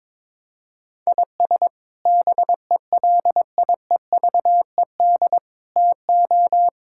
Morse code, heard as a keyed tone, about 22 wpm, 700 Hz. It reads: IS BELIEVED TO
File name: practice / 22wpm / Sets of 6 Words - 22wpm